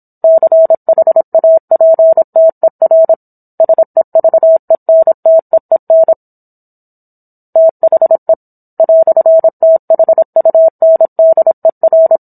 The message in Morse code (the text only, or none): CHAPTER SEVENTEEN THE "THUNDER